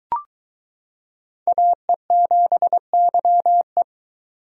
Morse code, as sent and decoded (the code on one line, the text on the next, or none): .- . --... -.-- .
AE7YE